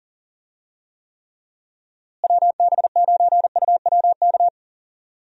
Morse code, heard as a keyed tone, about 40 wpm, 700 Hz. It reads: WB9UWK
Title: USA Call Signs - 6 Characters - Repeated 2x - 40wpm